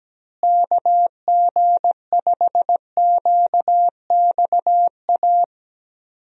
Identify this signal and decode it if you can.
Morse code, heard as a keyed tone, about 17 wpm, 700 Hz. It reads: KG5QXA
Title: USA Call Signs - 6 Characters - Repeated 3x - 17wpm